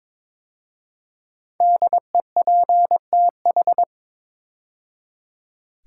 Morse code, heard as a keyed tone, about 22 wpm, 700 Hz. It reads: DEPTH
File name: 5 Letter Words - First 3 Letters Given - 22wpm